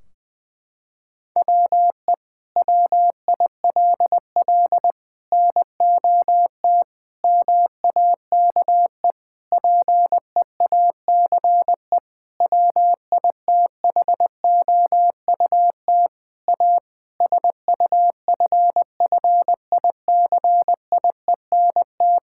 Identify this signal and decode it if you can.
Morse code, heard as a keyed tone, about 20 wpm, 700 Hz. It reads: WE WILL NOT MAKE PEACE WITHOUT A SUFFICIENT
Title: Sentences from Top 1600 Words 20wpm